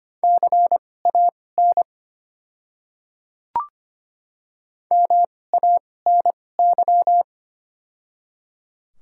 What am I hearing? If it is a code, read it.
Morse code, heard as a keyed tone, about 25 wpm, spaced as for 18 wpm, 700 Hz. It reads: CAN MANY
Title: 2x Character Spacing - Top 500 Words - 25wpm